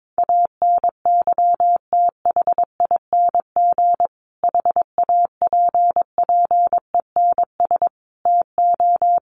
Morse code, heard as a keyed tone, about 22 wpm, 700 Hz. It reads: ANYTHING HAPPENS TO
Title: A Sound Of Thunder 22wpm